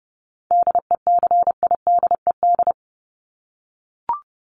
Morse code, heard as a keyed tone, about 30 wpm, 700 Hz. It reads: DECIDED
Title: ED Suffix - Top 100 Words with Base Words 30wpm